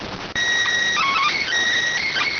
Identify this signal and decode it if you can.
SSTV (slow-scan television), PD 290.